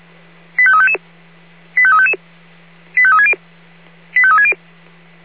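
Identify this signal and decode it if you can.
Five-tone selective call (ZVEI), heard as quick runs of five tones, tones of about 70 ms. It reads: 85355 85355 85360 85360
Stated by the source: ZVEI five-tone selective call 85355 received by SDR.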